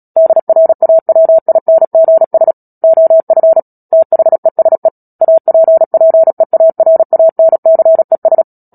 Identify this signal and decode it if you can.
Morse code, heard as a keyed tone, about 37 wpm, 650 Hz. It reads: DRAWINGS OF THESE APPEARANCES
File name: Book Two / 10-Chapter 10 - Epilogue - 36 WPM